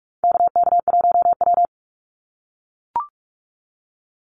Morse code, heard as a keyed tone, about 44 wpm, 700 Hz. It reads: KK1W
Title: NA SOTA Activator Callsigns 45wpm